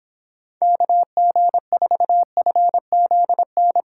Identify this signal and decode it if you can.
Morse code, heard as a keyed tone, about 26 wpm, 700 Hz. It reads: KG4FZN